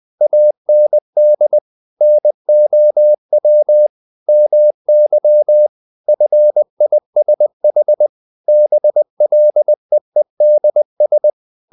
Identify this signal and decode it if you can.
Morse code, heard as a keyed tone, about 20 wpm, 600 Hz. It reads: AND NOW MY FISH BLEEDS
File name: the old man and the sea part 2 20wpm